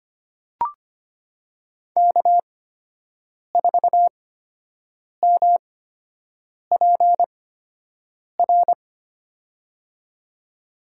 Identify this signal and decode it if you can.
Morse code, heard as a keyed tone, about 25 wpm, spaced as for 7 wpm, 700 Hz. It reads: K4MPR
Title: USA Call Signs - 5 Characters - 8x Character Spacing - 25wpm